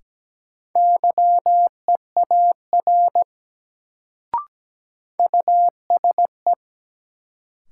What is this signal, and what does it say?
Morse code, heard as a keyed tone, about 17 wpm, 700 Hz. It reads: YEAR USE
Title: Sets of 2 Words - New Word Drill - 17wpm